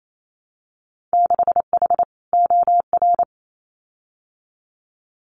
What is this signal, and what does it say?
Morse code, heard as a keyed tone, about 28 wpm, 700 Hz. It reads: 6H OR
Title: Winter Field Day Exchanges Encore 28wpm